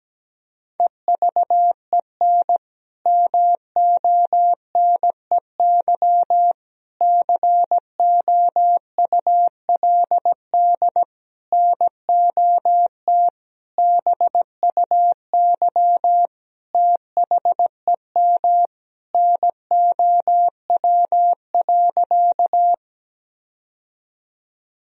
Morse code, heard as a keyed tone, about 17 wpm, 700 Hz. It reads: EVEN MONEY COULD NOT BUY THEM NOW.
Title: Sentences from Top 1000 Words - Review Part 4 - 17wpm